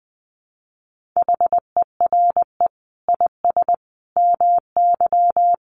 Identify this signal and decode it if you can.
Morse code, heard as a keyed tone, about 20 wpm, 700 Hz. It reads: HERE IS MY
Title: Sentences from Top 1400 Words 20wpm